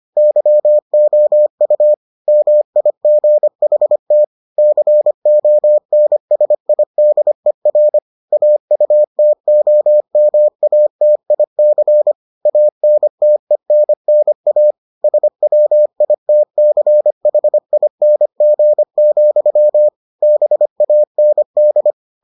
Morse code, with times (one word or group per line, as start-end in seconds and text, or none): 0.2-1.9 YOU
2.3-4.2 MIGHT
4.6-8.0 CONSIDER
8.3-12.1 AUTOMATIC
12.5-14.7 ANTENNA
15.0-19.9 SWITCHING,
20.2-21.9 BAND